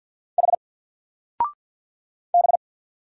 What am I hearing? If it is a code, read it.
Morse code, heard as a keyed tone, about 50 wpm, 700 Hz. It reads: H B
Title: Single Letter-Number 50wpm